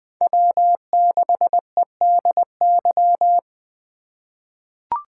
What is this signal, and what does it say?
Morse code, heard as a keyed tone, about 20 wpm, 700 Hz. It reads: W6EDY
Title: K1USN SST QSO Practice - 20wpm